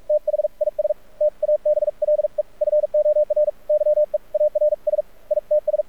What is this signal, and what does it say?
Morse code, heard as a keyed tone, about 46 wpm, spaced as for 39 wpm, 600 Hz. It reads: THIS TABLE FOR YEARS ITS
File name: archived / CRSnet